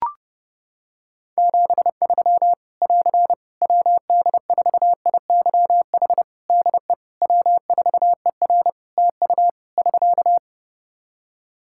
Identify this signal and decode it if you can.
Morse code, heard as a keyed tone, about 30 wpm, 700 Hz. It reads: 73 <AR> WD4IYH DE W4ER TU <SK>
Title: Practice QSO - Exchange 4 - Ending - 30wpm